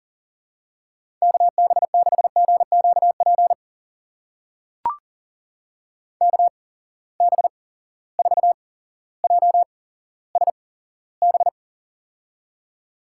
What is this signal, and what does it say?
Morse code, heard as a keyed tone, about 40 wpm, spaced as for 17 wpm, 700 Hz. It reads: KB6GQP KB4JSB
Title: USA Call Signs - 6 Characters - 8x Character Spacing - 40wpm